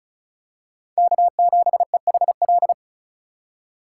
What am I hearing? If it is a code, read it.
Morse code, heard as a keyed tone, about 36 wpm, 700 Hz. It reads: K7EHL
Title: USA Call Signs - 5 Characters - 2x Character Spacing - 35wpm